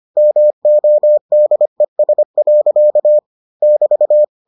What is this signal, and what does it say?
Morse code, heard as a keyed tone, about 25 wpm, 600 Hz. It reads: MODES. <BT>